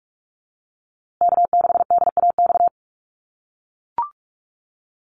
Morse code, heard as a keyed tone, about 44 wpm, 700 Hz. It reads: K6DAX
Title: USA Call Signs Repeated 3x 45wpm